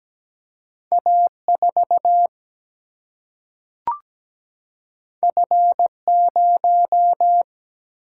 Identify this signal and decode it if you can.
Morse code, heard as a keyed tone, about 17 wpm, 700 Hz. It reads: A 4 F 0
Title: Single-Letter Single-Number 17wpm